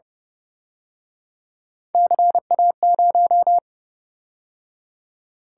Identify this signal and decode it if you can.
Morse code, heard as a keyed tone, about 32 wpm, 700 Hz. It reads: CA0